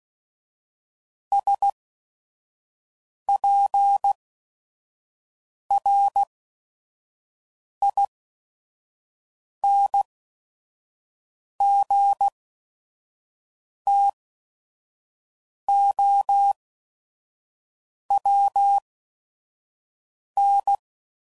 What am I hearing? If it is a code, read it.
Morse code, heard as a keyed tone, about 16 wpm, spaced as for 5 wpm, 800 Hz. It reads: SPRINGTOWN
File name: CW Code Groups @5WPM / fromSM's / Cities